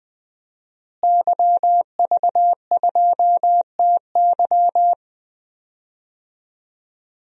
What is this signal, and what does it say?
Morse code, heard as a keyed tone, about 20 wpm, 700 Hz. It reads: YV2TY
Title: DXCC-100 Call Signs - 5 Characters - Repeated 3x - 20wpm